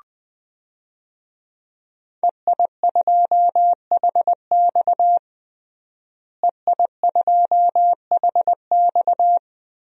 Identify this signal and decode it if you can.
Morse code, heard as a keyed tone, about 20 wpm, 700 Hz. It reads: EI2HX EI2HX
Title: DXCC-100 Call Signs - 5 Characters - Repeated 2x - 20wpm